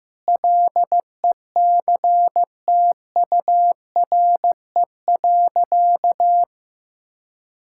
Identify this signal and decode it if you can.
Morse code, heard as a keyed tone, about 15 wpm, 700 Hz. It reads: LECTURE.
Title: Sentences from Top 1600 Words 15wpm